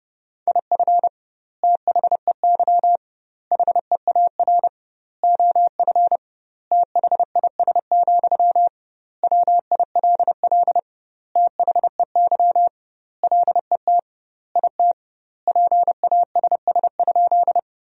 Morse code, heard as a keyed tone, about 30 wpm, 700 Hz. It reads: IF THEY HEAR OF THIS, WILL THEY LET IT PASS?
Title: 2x Word-Spacing - Sentences from Top 400 Words - 30wpm